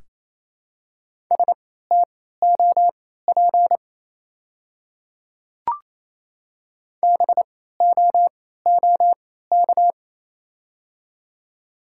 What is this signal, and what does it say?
Morse code, heard as a keyed tone, about 27 wpm, spaced as for 16 wpm, 700 Hz. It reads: STOP BOOK
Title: 3x Character Spacing - Top 300 Words - 28wpm